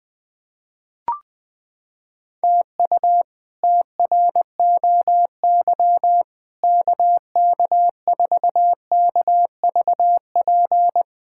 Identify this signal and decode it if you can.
Morse code, heard as a keyed tone, about 21 wpm, 700 Hz. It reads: TU TROY KK4KVP
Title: K1USN SST QSO Practice - 20wpm